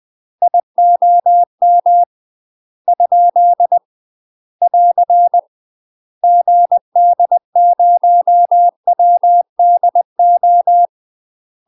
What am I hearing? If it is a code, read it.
Morse code, heard as a keyed tone, about 20 wpm, 700 Hz. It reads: IOM ? <AR> GD0WDO
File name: UK MASTER QSOs TRAINING/20WPM